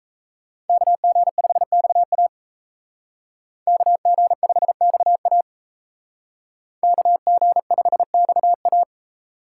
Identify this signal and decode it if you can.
Morse code, heard as a keyed tone, about 36 wpm, 700 Hz. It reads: KG5XA KG5XA KG5XA